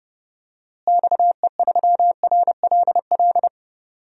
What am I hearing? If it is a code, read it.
Morse code, heard as a keyed tone, about 30 wpm, 700 Hz. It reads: XE3RLL
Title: DXCC-100 Call Signs - 6 Characters - Repeated 3x - 30wpm